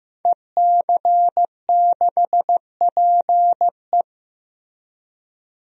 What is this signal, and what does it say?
Morse code, heard as a keyed tone, about 15 wpm, 700 Hz. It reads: EC6PE
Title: DXCC-100 Call Signs - 5 Characters - 15wpm